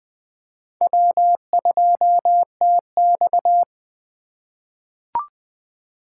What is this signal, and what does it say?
Morse code, heard as a keyed tone, about 20 wpm, 700 Hz. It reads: W2TX
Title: USA Call Signs - 3x Character Spacing - 20wpm